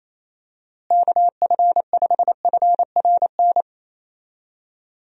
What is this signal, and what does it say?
Morse code, heard as a keyed tone, about 28 wpm, 700 Hz. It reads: KF5FRN
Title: USA Call Signs - 6 Characters - Repeated 2x - 28wpm